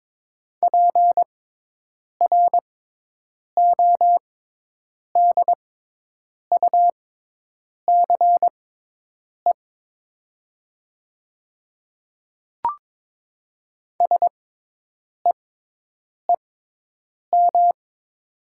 Morse code, heard as a keyed tone, about 22 wpm, spaced as for 8 wpm, 700 Hz. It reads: PRODUCE SEEM
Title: 6x Character Spacing - Top 500 Words - 22wpm